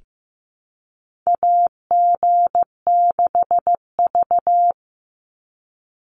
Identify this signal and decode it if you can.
Morse code, heard as a keyed tone, about 15 wpm, 700 Hz. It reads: AG6V
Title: NA SOTA Chaser Callsigns 15wpm